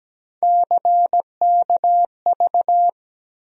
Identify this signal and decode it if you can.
Morse code, heard as a keyed tone, about 17 wpm, 700 Hz. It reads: CKV
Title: USA Call Sign Postfixes - 3 Letters 17wpm